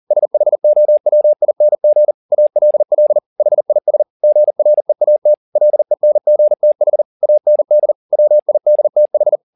Morse code, heard as a keyed tone, about 40 wpm, 600 Hz. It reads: SHOWING ALL HIS GREAT LENGTH AND WIDTH
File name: the old man and the sea 2 40wpm